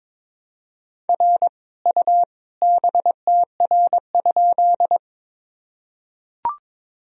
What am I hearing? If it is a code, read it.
Morse code, heard as a keyed tone, about 21 wpm, 700 Hz. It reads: R U BTR?
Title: Sentences from Top 400 Words and Abbreviations 22wpm